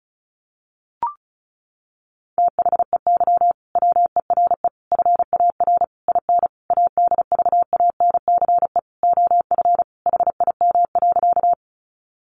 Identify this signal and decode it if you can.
Morse code, heard as a keyed tone, about 34 wpm, 700 Hz. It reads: THEY WERE FAR IN ADVANCE OF HIM.
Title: Sentences from Top 1400 Words 35wpm